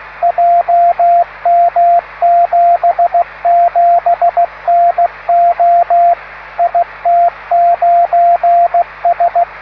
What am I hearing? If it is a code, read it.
Morse code, heard as a keyed tone, about 16 wpm, 690 Hz, with radio noise. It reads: JM77NO IT9S